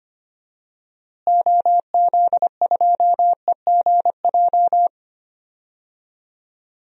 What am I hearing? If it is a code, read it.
Morse code, heard as a keyed tone, about 25 wpm, 700 Hz. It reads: OZ2EGJ
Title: DXCC-100 Call Signs - 6 Characters - Repeated 2x - 25wpm